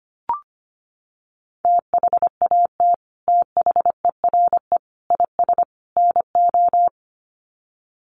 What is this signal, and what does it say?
Morse code, heard as a keyed tone, about 25 wpm, 700 Hz. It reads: THAT THERE IS NO
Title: Sets of 4 Words 25wpm